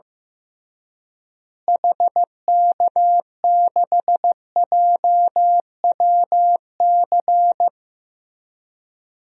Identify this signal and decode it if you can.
Morse code, heard as a keyed tone, about 15 wpm, 700 Hz. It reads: HK6JWC
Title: DXCC-100 Call Signs - 6 Characters - Repeated 3x - 15wpm